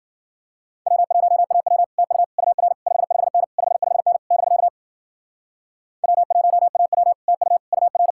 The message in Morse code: W1AW TU UR 55N 55N <BK> W1AW TU UR

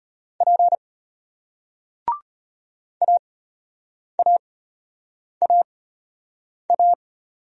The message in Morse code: P AAAA

